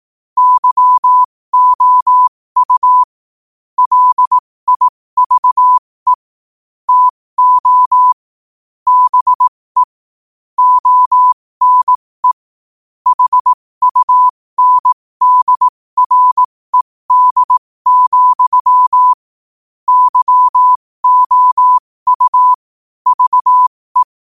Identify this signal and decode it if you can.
Morse code, heard as a keyed tone, about 18 wpm, spaced as for 15 wpm, 1000 Hz. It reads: YOU LIVE TO BE ONE HUNDRED, YOU VE
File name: qotdfiles / 15wpm